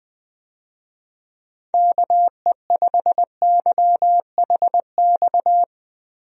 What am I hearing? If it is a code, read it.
Morse code, heard as a keyed tone, about 21 wpm, 700 Hz. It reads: KE5YHX
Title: USA Call Signs - 6 Characters - Repeated 3x - 20wpm